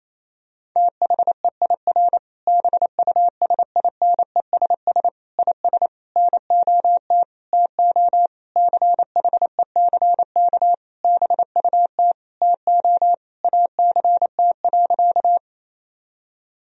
Morse code, heard as a keyed tone, about 28 wpm, 700 Hz. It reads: THEIR BUSINESS IS NOT TO CHECK BUT TO ACT.
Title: Sentences from Top 1000 Words - Review Part 3 - 28wpm